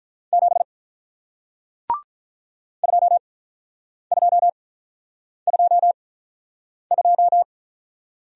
Morse code, heard as a keyed tone, about 47 wpm, spaced as for 9 wpm, 700 Hz. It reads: 7 2222